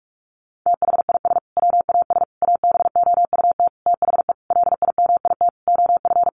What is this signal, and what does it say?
Morse code, heard as a keyed tone, about 44 wpm, 700 Hz. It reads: THIS WAS ABOUT THE LIMIT OF
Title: Sentences from Top 1500 Words 45wpm